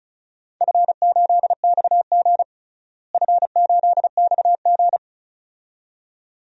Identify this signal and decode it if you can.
Morse code, heard as a keyed tone, about 36 wpm, 700 Hz. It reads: F8XG F8XG
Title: DXCC-100 Call Signs - 4 Characters - Repeated 2x - 35wpm